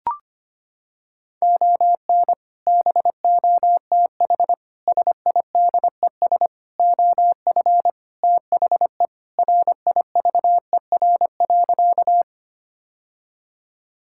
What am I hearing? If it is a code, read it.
Morse code, heard as a keyed tone, about 25 wpm, 700 Hz. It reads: ON BOTH SIDES OF THE RIVER.
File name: practice / 25wpm / Sentences with 4-Word Phrases - 25wpm